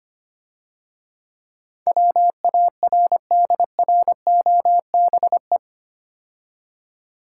Morse code, heard as a keyed tone, about 27 wpm, 700 Hz. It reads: WARDROBE
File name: Easy Words - Speed-Racing with Hints - 25wpm